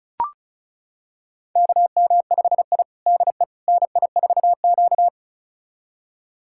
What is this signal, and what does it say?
Morse code, heard as a keyed tone, about 36 wpm, 700 Hz. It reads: KM5I DE NI4Q
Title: USA Callsign DE Callsign - 4 Characters - 35wpm